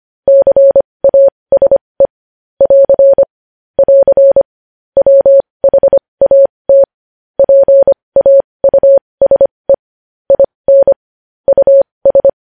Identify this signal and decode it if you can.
Morse code, heard as a keyed tone, about 25 wpm, spaced as for 20 wpm, 550 Hz. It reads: CASE <AR> <AR> WHAT PAUSE IN US
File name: podcast / News headlines-1x-25x20